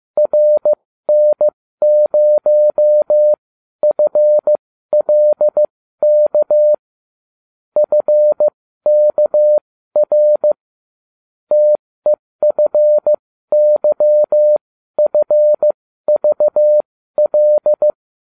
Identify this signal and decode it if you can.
Morse code, heard as a keyed tone, about 15 wpm, spaced as for 12 wpm, 600 Hz. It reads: RN0FLK FKR TEFYFVL